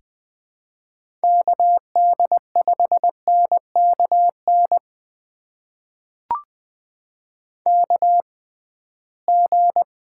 Morse code, heard as a keyed tone, about 20 wpm, spaced as for 15 wpm, 700 Hz. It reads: KD5NKN KG
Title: USA Call Signs - 6 Characters - 6x Character Spacing - 20wpm